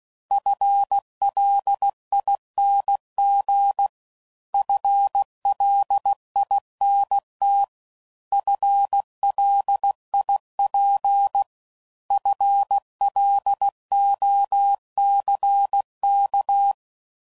Morse code, heard as a keyed tone, about 16 wpm, 800 Hz. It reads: FLING FLINT FLIP FLOCK
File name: Prefixes / FLs & THs / FL Runs